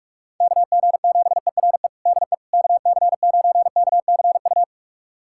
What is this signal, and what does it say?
Morse code, heard as a keyed tone, about 46 wpm, 700 Hz. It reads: KG7ERE DE KC9KCU
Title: USA Callsign DE Callsign 45wpm